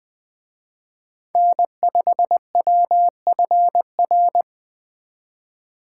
Morse code, heard as a keyed tone, about 24 wpm, 700 Hz. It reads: N5WFR
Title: Speed Racing - USA Call Signs 20wpm